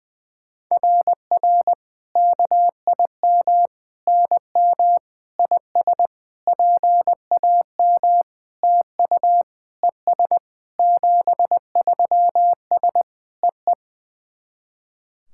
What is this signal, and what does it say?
Morse code, heard as a keyed tone, about 20 wpm, 700 Hz. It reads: RR KIM NM IS PAM TU ES 73S EE
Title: SOTA QSO Practice 20wpm